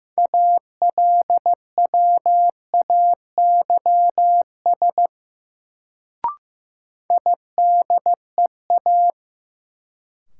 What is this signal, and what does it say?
Morse code, heard as a keyed tone, about 15 wpm, 700 Hz. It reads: ALWAYS IDEA
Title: Sentences from Top 300 Words - New Word Drill - 15wpm